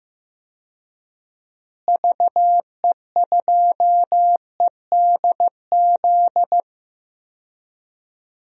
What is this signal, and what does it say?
Morse code, heard as a keyed tone, about 15 wpm, 700 Hz. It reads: VE2EDZ
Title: DXCC-100 Call Signs - 6 Characters - Repeated 3x - 15wpm